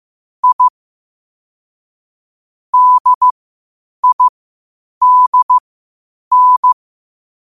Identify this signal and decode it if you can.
Morse code, heard as a keyed tone, about 15 wpm, spaced as for 8 wpm, 1000 Hz. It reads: I DIDN
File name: qotdfiles / 8wpm